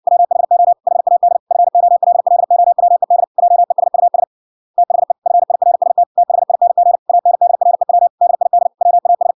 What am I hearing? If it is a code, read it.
Morse code, heard as a keyed tone, about 58 wpm, 700 Hz. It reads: WHO HAD FOLLOWED JESUS THE FIRST THING ANDREW DID WAS